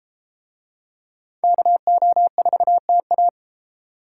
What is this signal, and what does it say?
Morse code, heard as a keyed tone, about 31 wpm, 700 Hz. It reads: KO4TA